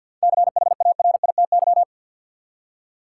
Morse code, heard as a keyed tone, about 53 wpm, 700 Hz. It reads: CHARITY
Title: Seven Letter Words - 50wpm